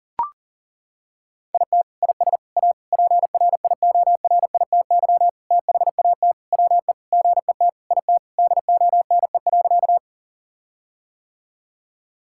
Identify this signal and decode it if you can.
Morse code, heard as a keyed tone, about 40 wpm, 700 Hz. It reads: IT IS A PRIORITY THAT WE GET IT DONE.
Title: Sentences from Top 1600 Words 40wpm